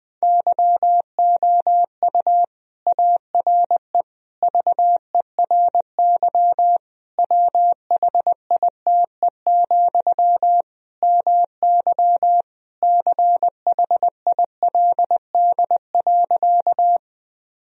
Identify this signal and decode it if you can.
Morse code, heard as a keyed tone, about 20 wpm, 700 Hz. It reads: YOU ARE VERY WHITE, MY CHILD.
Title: Sentences from Top 1000 Words - Review Part 1 - 20wpm